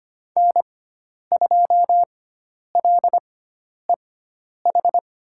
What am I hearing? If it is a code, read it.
Morse code, heard as a keyed tone, about 25 wpm, spaced as for 10 wpm, 700 Hz. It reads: N2LEH
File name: Lesson 16g - Letter c - Call Signs 25wpm